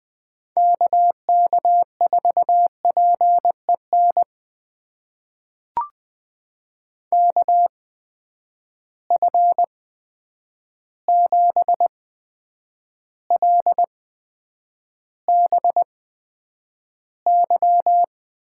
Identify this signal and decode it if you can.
Morse code, heard as a keyed tone, about 20 wpm, spaced as for 9 wpm, 700 Hz. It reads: KK4PEN KF7LBY
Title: USA Call Signs - 6 Characters - 8x Character Spacing - 20wpm